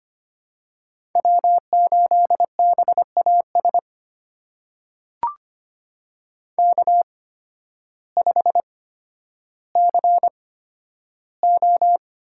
Morse code, heard as a keyed tone, about 25 wpm, spaced as for 12 wpm, 700 Hz. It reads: W8BAS K5CO
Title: USA Call Signs - 5 Characters - 8x Character Spacing - 25wpm